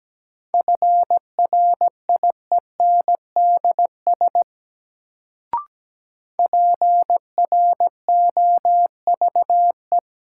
Morse code, heard as a keyed tone, about 17 wpm, 700 Hz. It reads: FRIENDS PROVE
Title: Sentences from Top 1600 Words - New Word Drill Alt - 17wpm